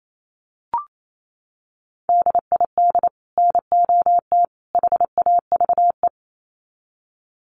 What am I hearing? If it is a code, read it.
Morse code, heard as a keyed tone, about 28 wpm, 700 Hz. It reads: DID NOT HAVE
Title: Sets of 3 Words 28wpm